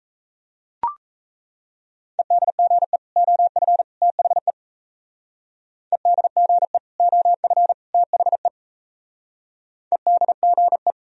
Morse code, heard as a keyed tone, about 34 wpm, 700 Hz. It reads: EDGE OF THE EDGE OF THE EDGE
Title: Speed-Racing - Sets of 3 Words 28wpm